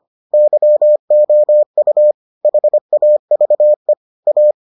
Morse code, heard as a keyed tone, about 25 wpm, 600 Hz. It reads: YOU HAVE A